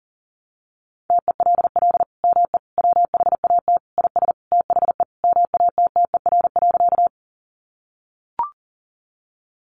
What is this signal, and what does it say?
Morse code, heard as a keyed tone, about 40 wpm, 700 Hz. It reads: TELL ME WHAT IS THE MATTER.
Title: Sentences from Top 700 Words 40wpm